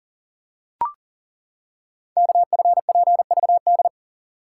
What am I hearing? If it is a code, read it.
Morse code, heard as a keyed tone, about 42 wpm, 690 Hz. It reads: KFPVD
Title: Code Groups - 5 Letters - 40wpm